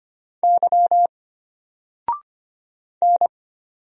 Morse code, heard as a keyed tone, about 25 wpm, 700 Hz. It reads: Y N
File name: Single-Letter-Number-25wpm